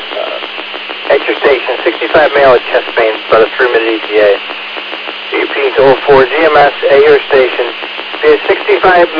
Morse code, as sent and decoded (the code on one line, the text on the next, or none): none